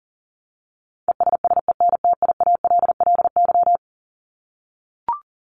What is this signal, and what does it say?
Morse code, heard as a keyed tone, about 40 wpm, 700 Hz. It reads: ESSENTIALLY
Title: LY Suffix - Top 100 Words with Base Words 40wpm